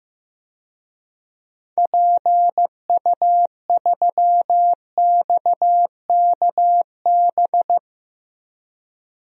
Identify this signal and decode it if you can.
Morse code, heard as a keyed tone, about 16 wpm, 700 Hz. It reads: PU3XKB
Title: DXCC-100 Call Signs - 6 Characters - Repeated 2x - 15wpm